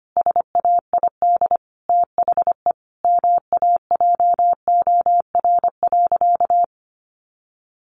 Morse code, heard as a keyed tone, about 25 wpm, 700 Hz. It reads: SAID THE MAJOR.
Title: Sentences from Top 600 Words 25wpm